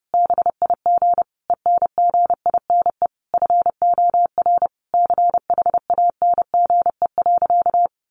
Morse code, heard as a keyed tone, about 30 wpm, 700 Hz. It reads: BIG ENGINE FOR CHANGE.
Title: Sentences from Top 1500 Words 30wpm